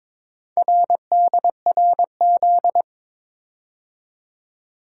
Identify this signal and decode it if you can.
Morse code, heard as a keyed tone, about 23 wpm, 700 Hz. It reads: RDRZ
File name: practice / 22wpm / Four Letter-Number Combinations - 22wpm